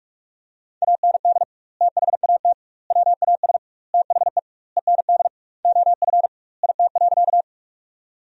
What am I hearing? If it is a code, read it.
Morse code, heard as a keyed tone, about 41 wpm, 700 Hz. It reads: AND THAT WAS THE END OF IT.
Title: 2x Word-Spacing - Sentences from Top 200 Words - 45wpm